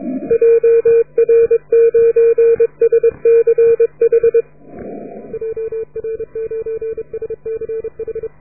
Morse code, heard as a keyed tone, about 28 wpm, 450 Hz. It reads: JR9SCH JR9SCH